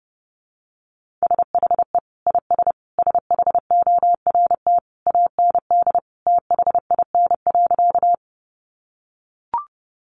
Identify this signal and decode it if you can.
Morse code, heard as a keyed tone, about 30 wpm, 700 Hz. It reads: SHE IS SHORT AND THIN.
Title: Sentences from Top 1200 Words 30wpm